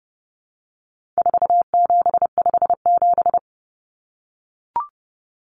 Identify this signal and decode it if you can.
Morse code, heard as a keyed tone, about 30 wpm, 700 Hz. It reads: 4757
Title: Numbers - 4-Digit 30wpm